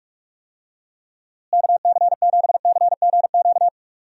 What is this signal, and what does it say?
Morse code, heard as a keyed tone, about 46 wpm, 700 Hz. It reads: KC7CGQ
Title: USA Call Signs - 6 Characters - Repeated 2x - 45wpm